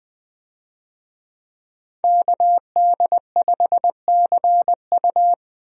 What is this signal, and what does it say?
Morse code, heard as a keyed tone, about 21 wpm, 700 Hz. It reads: KD5CU
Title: USA Call Signs - 5 Characters - Repeated 3x - 20wpm